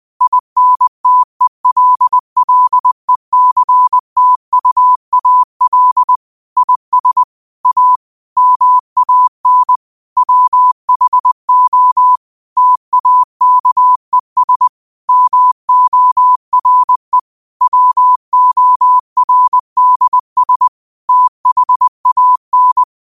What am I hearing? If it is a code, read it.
Morse code, heard as a keyed tone, about 20 wpm, 1000 Hz. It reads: INTELLECTUAL IS A MAN WHO TAKES MORE WORDS THAN
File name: qotdfiles / 20wpm